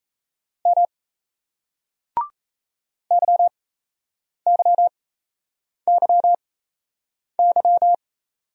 Morse code, heard as a keyed tone, about 34 wpm, spaced as for 8 wpm, 700 Hz. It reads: M YYYY
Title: Speed-Racing Single Letters 28wpm